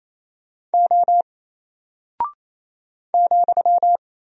Morse code, heard as a keyed tone, about 28 wpm, 700 Hz. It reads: O ,
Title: Single Character (Letter:Number:Punctuation:Prosign) Alt 28wpm